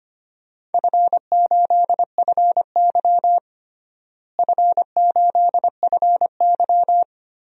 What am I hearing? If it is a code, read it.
Morse code, heard as a keyed tone, about 25 wpm, 700 Hz. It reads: F8FY F8FY
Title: DXCC-100 Call Signs - 4 Characters - Repeated 3x - 25wpm